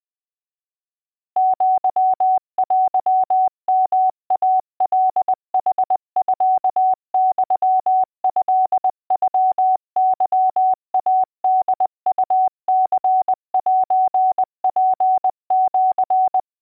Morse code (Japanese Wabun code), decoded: アテヨイカヌミユトノケイホウニセツシ